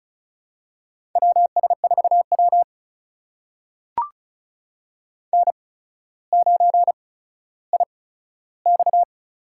Morse code, heard as a keyed tone, about 36 wpm, spaced as for 15 wpm, 700 Hz. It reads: WS4W N9IX
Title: USA Call Signs - 4 Characters - 8x Character Spacing - 35wpm